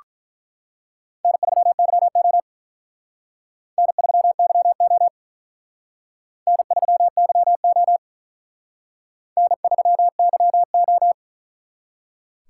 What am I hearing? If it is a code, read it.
Morse code, heard as a keyed tone, about 43 wpm, 700 Hz. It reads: N3YO N3YO N3YO N3YO
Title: Speed Racing - USA 1x2 Call Signs 35wpm